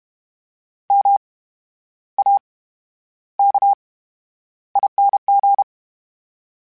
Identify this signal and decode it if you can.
Morse code, heard as a keyed tone, about 32 wpm, spaced as for 11 wpm, 800 Hz. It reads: MAKING
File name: Special INGs / Special INGs-mp3s / VerySlowToFaster Ing @30